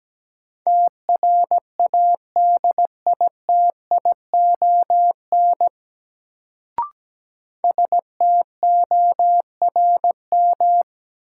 Morse code, heard as a keyed tone, about 17 wpm, 700 Hz. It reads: TRADITION STORM
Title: Sentences from Top 1500 Words - New Word Drill - 17wpm